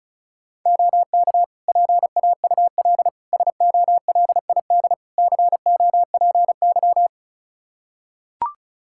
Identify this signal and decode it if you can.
Morse code, heard as a keyed tone, about 36 wpm, 700 Hz. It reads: OK PAUL SOLID COPY
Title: K1USN SST QSO Practice 35wpm